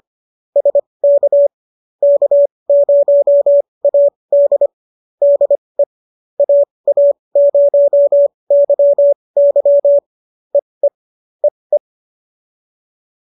Morse code, heard as a keyed tone, about 25 wpm, spaced as for 20 wpm, 570 Hz. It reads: SK K0AD DE AA0YY EE EE